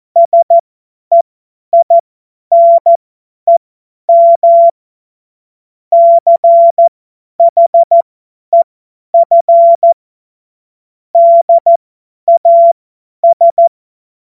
Morse code, heard as a keyed tone, about 14 wpm, spaced as for 10 wpm, 680 Hz. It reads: SEINEM CHEF DAS
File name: Morsekurs / Witze / Witz 20. Dezember 2022 50 F70